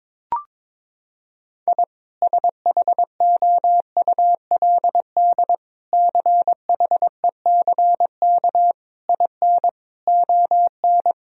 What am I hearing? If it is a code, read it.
Morse code, heard as a keyed tone, about 22 wpm, 700 Hz. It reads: I SHOULD CHECK IN ON
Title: Sentences from Top 1000 Words - Review Part 3 - 22wpm